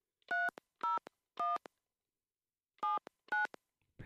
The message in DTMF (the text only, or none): 301 79